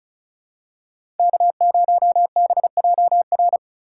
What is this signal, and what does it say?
Morse code, heard as a keyed tone, about 34 wpm, 700 Hz. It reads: K0BJR
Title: USA Call Signs - 6x Character Spacing - 35wpm